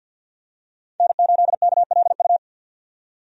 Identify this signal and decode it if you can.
Morse code, heard as a keyed tone, about 51 wpm, 700 Hz. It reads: N8KLU
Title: USA Call Signs - 5 Characters - 50wpm